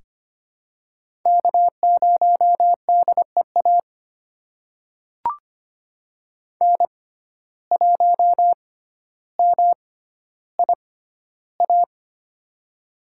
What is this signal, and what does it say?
Morse code, heard as a keyed tone, about 25 wpm, spaced as for 13 wpm, 700 Hz. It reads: K0DEA N1MIA
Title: USA Call Signs - 6x Character Spacing - 25wpm